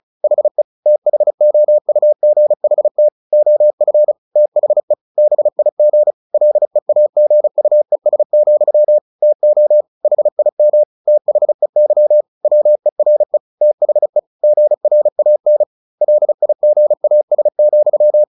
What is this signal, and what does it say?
Morse code, heard as a keyed tone, about 34 wpm, 600 Hz. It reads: HE THOUGHT OF THE BIG LEAGUES, TO HIM THEY WERE THE GRAN LIGAS,
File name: the old man and the sea / the old man and the sea 1 35wpm